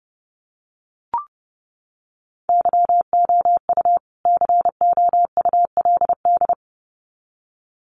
Morse code, heard as a keyed tone, about 30 wpm, 700 Hz. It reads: YOU COULD
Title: Sets of 2 Words 30wpm